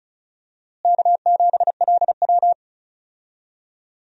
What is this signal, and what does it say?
Morse code, heard as a keyed tone, about 34 wpm, 700 Hz. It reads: K7LW
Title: USA Call Signs - 4 Characters - Repeated 3x - 35wpm